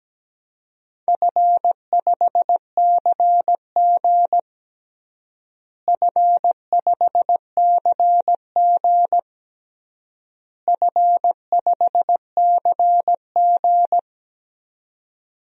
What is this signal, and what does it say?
Morse code, heard as a keyed tone, about 17 wpm, 700 Hz. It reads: F5CG F5CG F5CG
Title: DXCC-100 Call Signs - 4 Characters - Repeated 3x - 17wpm